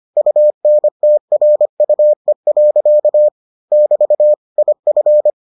Morse code, heard as a keyed tone, about 25 wpm, 600 Hz. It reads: UNTRUE. <BT> IF